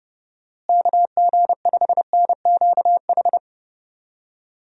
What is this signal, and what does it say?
Morse code, heard as a keyed tone, about 32 wpm, 700 Hz. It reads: KG5NQH
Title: USA Call Signs - 6 Characters - Repeated 2x - 30wpm